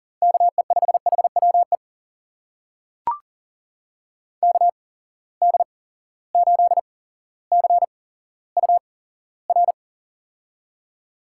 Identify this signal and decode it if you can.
Morse code, heard as a keyed tone, about 40 wpm, spaced as for 17 wpm, 700 Hz. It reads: KE5HWE KD8CUR
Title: USA Call Signs - 6 Characters - 8x Character Spacing - 40wpm